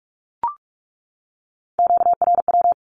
Morse code, heard as a keyed tone, about 44 wpm, 700 Hz. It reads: QRW
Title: Q-Codes 45wpm